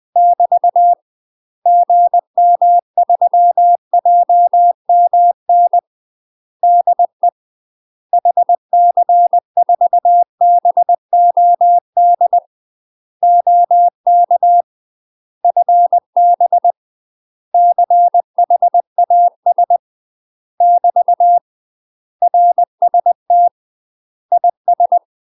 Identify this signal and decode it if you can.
Morse code, heard as a keyed tone, about 20 wpm, 700 Hz. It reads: <BT> GM3JMN DE HC4BOD OK FB CHAS <BT> RST IS
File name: UK MASTER QSOs TRAINING / 20WPM / UK MASTER QSOs 0073